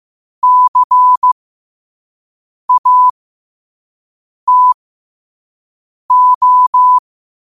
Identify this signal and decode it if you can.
Morse code, heard as a keyed tone, about 15 wpm, spaced as for 5 wpm, 1000 Hz. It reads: CATO
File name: 5wpm